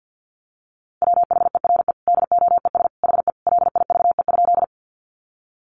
Morse code, heard as a keyed tone, about 50 wpm, 700 Hz. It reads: WHERE DOES HE LIVE?
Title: Sentences from Top 200 Words 50wpm